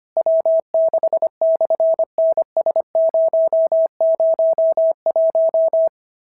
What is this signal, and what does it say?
Morse code, heard as a keyed tone, about 25 wpm, 690 Hz. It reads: W6/NS001
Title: SOTA QSO Practice 25wpm